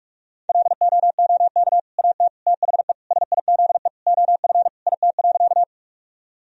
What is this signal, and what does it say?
Morse code, heard as a keyed tone, about 41 wpm, 700 Hz. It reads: LOOK AT THE SIZE OF IT.
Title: Sentences from Top 1000 Words - Review Part 2 - 45wpm